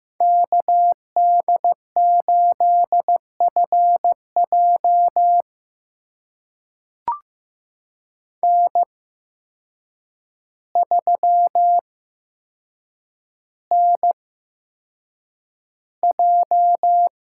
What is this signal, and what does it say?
Morse code, heard as a keyed tone, about 15 wpm, spaced as for 7 wpm, 700 Hz. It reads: KD8FJ N3NJ
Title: USA Call Signs - 5 Characters - 8x Character Spacing - 15wpm